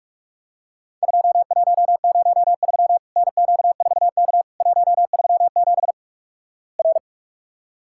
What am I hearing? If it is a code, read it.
Morse code, heard as a keyed tone, about 47 wpm, 700 Hz. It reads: 2103 NQ4K 137 R
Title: Worked All Europe DX Contest - QTC Exchange - 45wpm